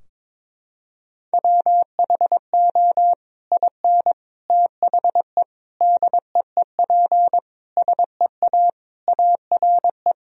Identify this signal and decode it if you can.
Morse code, heard as a keyed tone, about 21 wpm, 700 Hz. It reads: WHO IN THE DEEP SEA ARE
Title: Sentences from Top 1000 Words - Review Part 3 - 22wpm